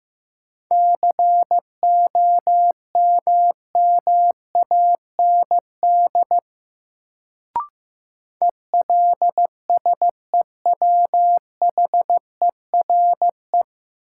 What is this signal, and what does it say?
Morse code, heard as a keyed tone, about 15 wpm, 700 Hz. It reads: COMMAND ELSEWHERE
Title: Sentences from Top 1600 Words - New Word Drill - 15wpm